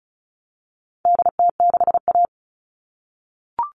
Morse code, heard as a keyed tone, about 34 wpm, 700 Hz. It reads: DT6A
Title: All Cut Numbers - 4-Digit 35wpm